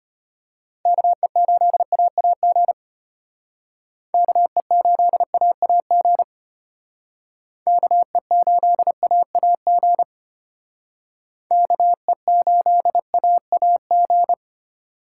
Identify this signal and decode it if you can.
Morse code, heard as a keyed tone, about 31 wpm, 700 Hz. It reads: KE8AAG KE8AAG KE8AAG KE8AAG
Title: Speed Racing - USA Call Signs 25wpm